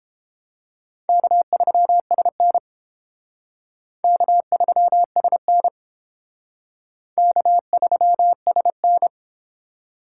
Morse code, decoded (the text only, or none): K3SN K3SN K3SN